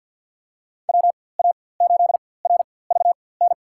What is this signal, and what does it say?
Morse code, heard as a keyed tone, about 52 wpm, spaced as for 23 wpm, 700 Hz. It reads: WA8RVN
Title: USA Call Signs - 6 Characters - 4x Character Spacing - 50wpm